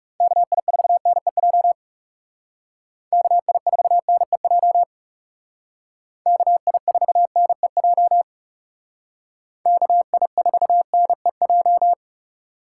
Morse code, heard as a keyed tone, about 36 wpm, 700 Hz. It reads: KI4NEJ KI4NEJ KI4NEJ KI4NEJ
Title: Speed Racing - USA Call Signs 30wpm